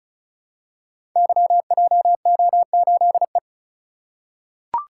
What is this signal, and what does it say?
Morse code, heard as a keyed tone, about 34 wpm, 690 Hz. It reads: YJO8E
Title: Code Groups - 5 Letter:Numbers - 35wpm